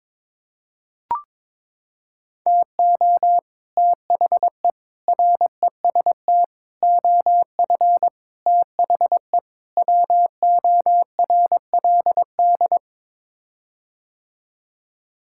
Morse code, heard as a keyed tone, about 21 wpm, 700 Hz. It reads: TO THE REST OF THE WORLD
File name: practice / 22wpm / Sets of 6 Words - 22wpm